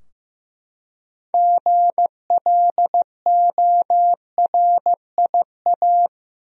Morse code, heard as a keyed tone, about 15 wpm, 700 Hz. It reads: GLORIA
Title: USA Common Names - 15wpm